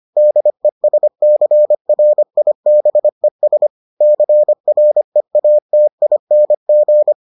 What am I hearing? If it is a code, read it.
Morse code, heard as a keyed tone, about 25 wpm, 600 Hz. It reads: DESCRIBES CREATING